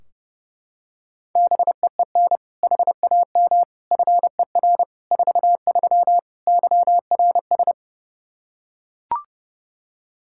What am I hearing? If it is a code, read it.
Morse code, heard as a keyed tone, about 32 wpm, 700 Hz. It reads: BEEN HAM FER 43 YRS
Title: Practice QSO - Exchange 3 - More Info - 30wpm